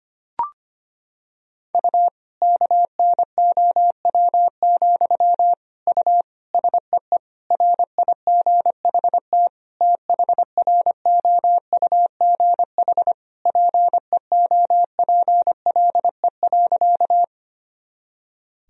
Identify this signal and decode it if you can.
Morse code, heard as a keyed tone, about 25 wpm, 700 Hz. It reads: U KNOW, U SEE RIGHT THROUGH PEOPLE.
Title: Sentences from Top 200 Words and Abbreviations 25wpm